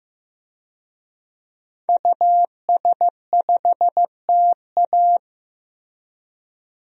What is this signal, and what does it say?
Morse code, heard as a keyed tone, about 15 wpm, 700 Hz. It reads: US5TA
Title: DXCC-100 Call Signs - 5 Characters - 15wpm